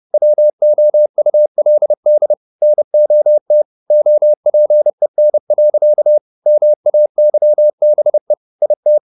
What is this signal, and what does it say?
Morse code, heard as a keyed tone, about 30 wpm, 600 Hz. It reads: WOULD NOT OPEN. MAYBE IT
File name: the old man and the sea / the old man and the sea 1 30wpm